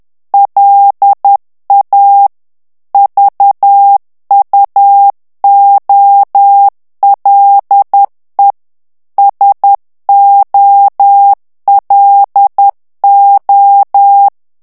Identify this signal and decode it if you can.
Morse code, heard as a keyed tone, about 10 wpm, 800 Hz. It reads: LA VUOLE SOLO
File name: morse mp3 / ITALIEN texte clair 1 vitesse 10 mots minute